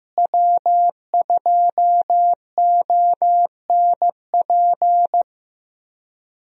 Morse code, heard as a keyed tone, about 15 wpm, 700 Hz. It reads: W2ONP
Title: USA Call Signs - 5 Characters - Repeated 3x - 15wpm